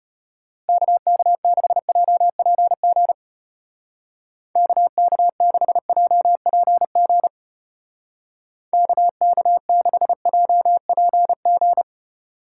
Morse code, heard as a keyed tone, about 31 wpm, 700 Hz. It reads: KK6JPG KK6JPG KK6JPG